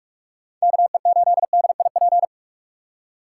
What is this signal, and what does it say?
Morse code, heard as a keyed tone, about 41 wpm, 700 Hz. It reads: KE8DIP